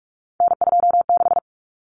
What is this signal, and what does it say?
Morse code, heard as a keyed tone, about 44 wpm, 700 Hz. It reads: N26
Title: Common Cut Numbers - 3-Digit 45wpm